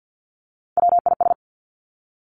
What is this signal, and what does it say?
Morse code, heard as a keyed tone, about 50 wpm, 700 Hz. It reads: WIS
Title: Three Challenging Letter-Number Combinations 50wpm